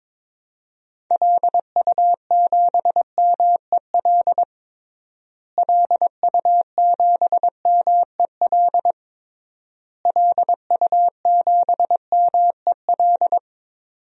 Morse code, heard as a keyed tone, about 22 wpm, 700 Hz. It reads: LU7MEL LU7MEL LU7MEL
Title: DXCC-100 Call Signs - 6 Characters - Repeated 3x - 22wpm